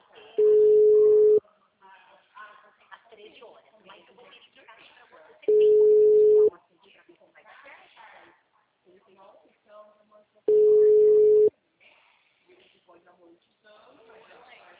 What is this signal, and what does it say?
Ringback tone (Europe, 425 Hz), with speech.